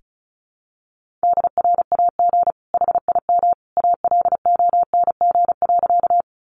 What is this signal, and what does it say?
Morse code, heard as a keyed tone, about 34 wpm, 700 Hz. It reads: DRAG HIM ALONG.
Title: Sentences from Top 1500 Words 35wpm